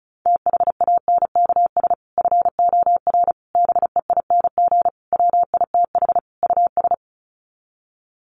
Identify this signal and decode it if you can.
Morse code, heard as a keyed tone, about 34 wpm, 700 Hz. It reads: THANKS FOR BEING WITH US
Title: Sets of 5 Words 35wpm